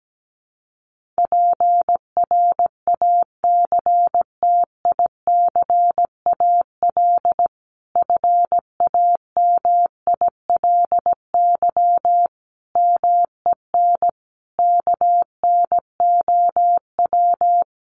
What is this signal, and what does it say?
Morse code, heard as a keyed tone, about 17 wpm, 700 Hz. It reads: PRACTICAL FAMILY MEN KNOW
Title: Sentences from Top 1400 Words 17wpm